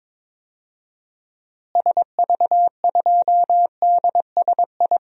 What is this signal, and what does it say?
Morse code, heard as a keyed tone, about 22 wpm, 700 Hz. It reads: SV2DSI
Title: DXCC-100 Call Signs - 6 Characters - Repeated 3x - 22wpm